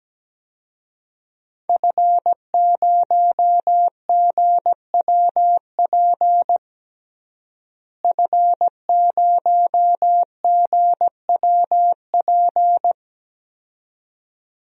Morse code, heard as a keyed tone, about 17 wpm, 700 Hz. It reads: F0GWP F0GWP
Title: DXCC-100 Call Signs - 5 Characters - Repeated 2x - 17wpm